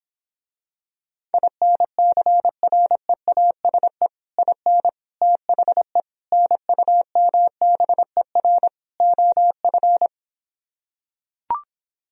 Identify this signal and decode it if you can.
Morse code, heard as a keyed tone, about 26 wpm, 700 Hz. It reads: INCREASE IN THE NUMBER OF